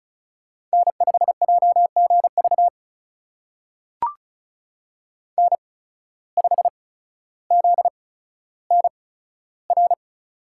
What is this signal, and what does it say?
Morse code, heard as a keyed tone, about 36 wpm, spaced as for 15 wpm, 700 Hz. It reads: N5JGV N5ZNR
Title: USA Call Signs - 5 Characters - 8x Character Spacing - 35wpm